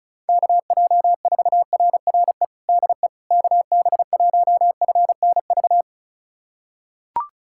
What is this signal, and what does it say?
Morse code, heard as a keyed tone, about 36 wpm, 700 Hz. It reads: KJ4RRE DE KB1FNV
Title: USA Callsign DE Callsign - 6 Characters - 35wpm